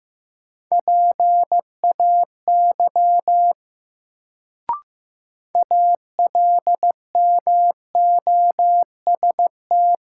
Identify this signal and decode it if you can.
Morse code, heard as a keyed tone, about 15 wpm, 700 Hz. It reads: PAY ALMOST
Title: Sentences from Top 700 Words - New Word Drill - 15wpm